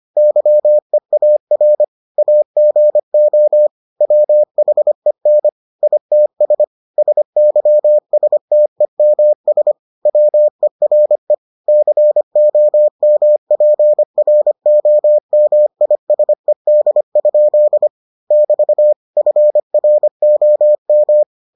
Morse code, heard as a keyed tone, about 25 wpm, 600 Hz. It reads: YEAR AGO WHEN ITS SYSTEMS WERE COMPROMISED? <BT> FROM